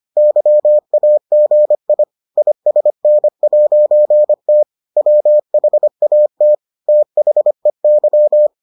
Morse code, heard as a keyed tone, about 25 wpm, 600 Hz. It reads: YAGI ISN'T WHAT THEY